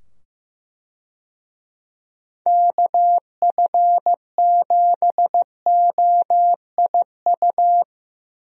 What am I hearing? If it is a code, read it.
Morse code, heard as a keyed tone, about 15 wpm, 700 Hz. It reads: KF7OIU
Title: USA Call Signs - 6 Characters - 15wpm